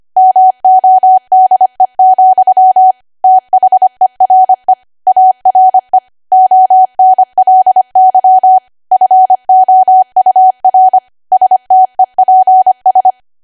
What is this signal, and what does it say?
Morse code, heard as a keyed tone, about 25 wpm, 750 Hz. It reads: MODE, THERE ARE ONLY FOUR STEPS
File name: Archive / 25 WPM / 25WPM